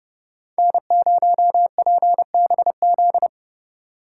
Morse code, heard as a keyed tone, about 29 wpm, 700 Hz. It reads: N0PBZ